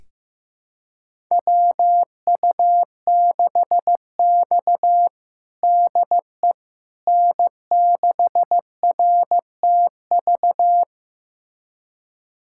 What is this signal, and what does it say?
Morse code, heard as a keyed tone, about 15 wpm, 700 Hz. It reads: WU6X DE N6RTV
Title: USA Callsign DE Callsign 15wpm